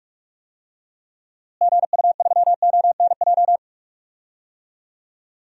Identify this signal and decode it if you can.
Morse code, heard as a keyed tone, about 42 wpm, 700 Hz. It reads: GU3ONJ
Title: DXCC-100 Call Signs - 6 Characters - Repeated 2x - 45wpm